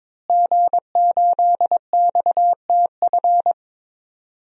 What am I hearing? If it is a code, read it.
Morse code, heard as a keyed tone, about 21 wpm, 700 Hz. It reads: G8XTF
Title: DXCC-100 Call Signs - 5 Characters - Repeated 3x - 22wpm